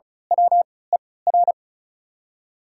Morse code, heard as a keyed tone, about 36 wpm, spaced as for 20 wpm, 700 Hz. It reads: WER
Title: 3x Character Spacing - Top 300 Words - 35wpm